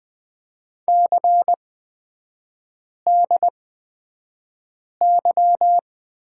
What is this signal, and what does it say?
Morse code, heard as a keyed tone, about 20 wpm, spaced as for 5 wpm, 700 Hz. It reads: CDY
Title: 250 Lesson 40e - BK - Three Characters 20wpm